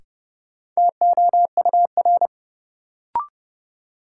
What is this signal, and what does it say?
Morse code, heard as a keyed tone, about 32 wpm, 700 Hz. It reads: TOUR
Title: Sentences from Top 1400 Words - New Word Drill Alt - 30wpm